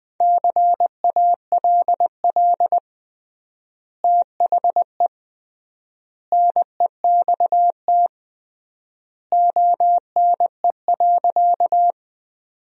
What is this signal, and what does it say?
Morse code, heard as a keyed tone, about 21 wpm, 700 Hz. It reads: CALL THE NEXT ONE.
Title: 3x Word-Spacing - Sentences from Top 300 Words - 20wpm